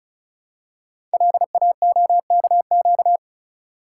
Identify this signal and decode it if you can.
Morse code, heard as a keyed tone, about 36 wpm, 700 Hz. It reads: LAOKQ
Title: Code Groups - 5 Letters - 35wpm